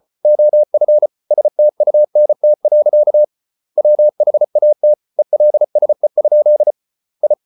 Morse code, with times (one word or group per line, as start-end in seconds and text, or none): 0.2-1.1 OF
1.3-3.2 STUNT.
3.8-4.9 WHAT
5.2-6.7 ELSE?
7.2-7.3 I